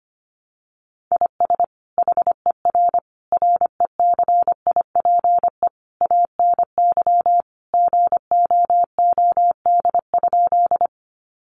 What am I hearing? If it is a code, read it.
Morse code, heard as a keyed tone, about 25 wpm, 700 Hz. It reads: IS HER RECIPE ANY GOOD?
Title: Sentences from Top 1200 Words 25wpm